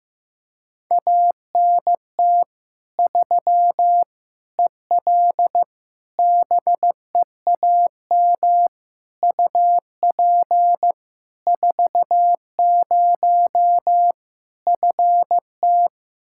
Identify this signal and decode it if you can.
Morse code, heard as a keyed tone, about 15 wpm, 700 Hz. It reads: ANT 3 EL BEAM UP 40 FT
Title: K1USN SST QSO Practice 15wpm